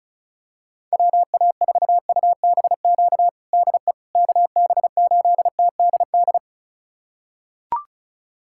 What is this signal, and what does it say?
Morse code, heard as a keyed tone, about 36 wpm, 700 Hz. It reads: WA4UBQ DE KB8TDD
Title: USA Callsign DE Callsign - 6 Characters - 35wpm